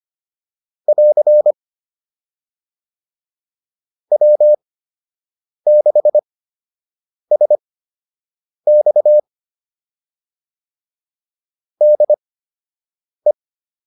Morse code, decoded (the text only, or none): <AR> W6SX DE